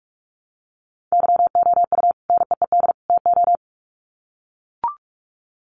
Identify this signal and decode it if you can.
Morse code, heard as a keyed tone, about 44 wpm, 700 Hz. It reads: YOU NEED TO
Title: Sets of 3 Words 45wpm